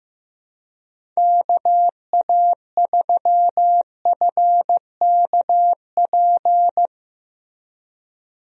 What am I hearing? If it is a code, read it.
Morse code, heard as a keyed tone, about 15 wpm, 700 Hz. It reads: KA3FKP
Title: USA Call Signs - 6 Characters - Repeated 3x - 15wpm